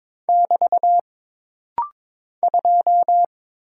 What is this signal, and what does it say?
Morse code, heard as a keyed tone, about 22 wpm, 700 Hz. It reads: <BT> 2
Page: This rapid-fire timing is geared to push your ICR (Instant Character Recognition) skills.